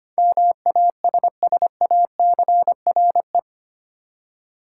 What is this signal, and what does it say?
Morse code, heard as a keyed tone, about 25 wpm, 700 Hz. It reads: MASSACRE
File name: 8 Letter Words - First 3 Letters Given - 25wpm